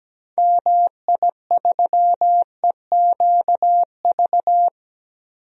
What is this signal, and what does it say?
Morse code, heard as a keyed tone, about 17 wpm, 700 Hz. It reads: MI3EQV
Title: DXCC-100 Call Signs - 6 Characters - Repeated 3x - 17wpm